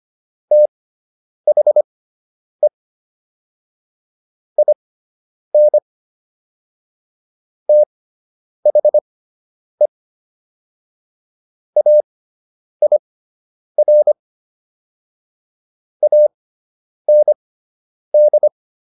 Morse code, heard as a keyed tone, about 25 wpm, spaced as for 9 wpm, 600 Hz. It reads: THE IN THE AIR AND